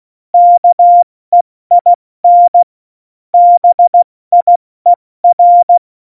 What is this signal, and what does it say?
Morse code, heard as a keyed tone, about 16 wpm, 700 Hz. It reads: KEIN BIER
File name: Morsekurs / Witze / Witz 10.Dezember 2019, 70 F80